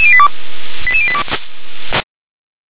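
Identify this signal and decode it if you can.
Five-tone selective call (ZVEI), heard as quick runs of five tones, tones of about 70 ms. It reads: 88082
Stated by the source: ZVEI five-tone selective call received on SDR.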